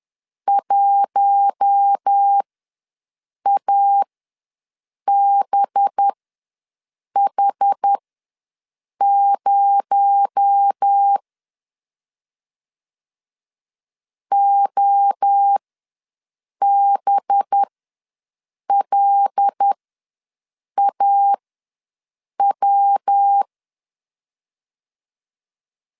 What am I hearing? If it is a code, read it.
Morse code, heard as a keyed tone, about 10 wpm, spaced as for 6 wpm, 800 Hz. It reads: 1ABH0 OBLAW
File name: LEZIONI / 7 GF9Z / 7 G ws 10wpm 3843